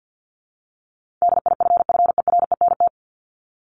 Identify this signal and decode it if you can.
Morse code, heard as a keyed tone, about 50 wpm, 700 Hz. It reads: DIFFERENT
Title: Top 501-750 Words 50wpm